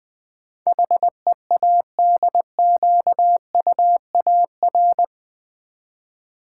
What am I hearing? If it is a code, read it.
Morse code, heard as a keyed tone, about 20 wpm, 700 Hz. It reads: HEADQUAR
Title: Build A Word - 4 Letters At A Time - 12-Letter Words - 20wpm